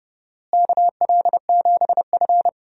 Morse code, heard as a keyed tone, about 30 wpm, 700 Hz. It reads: KL7F
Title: DXCC-100 Call Signs - 4 Characters - Repeated 3x - 30wpm